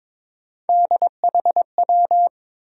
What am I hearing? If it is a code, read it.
Morse code, heard as a keyed tone, about 22 wpm, 700 Hz. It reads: DHW
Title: USA Call Sign Postfixes - 3 Letters 22wpm